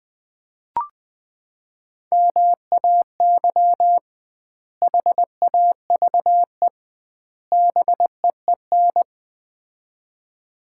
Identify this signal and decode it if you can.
Morse code, heard as a keyed tone, about 20 wpm, 700 Hz. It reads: MAY HAVE BEEN
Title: 2x Word-Spacing - Sets of 3 Words - 20wpm